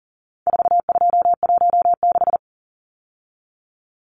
Morse code, heard as a keyed tone, about 40 wpm, 700 Hz. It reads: 4216
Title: Numbers - 4-Digit 40wpm